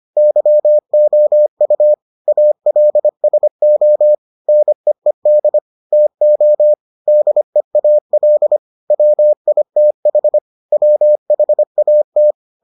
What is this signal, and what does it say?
Morse code, heard as a keyed tone, about 25 wpm, 600 Hz. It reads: YOU ALSO NEED TO DEAL WITH WHAT